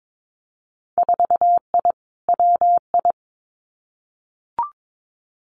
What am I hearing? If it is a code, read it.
Morse code, heard as a keyed tone, about 22 wpm, 700 Hz. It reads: 4I WI
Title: Winter Field Day Exchanges 22wpm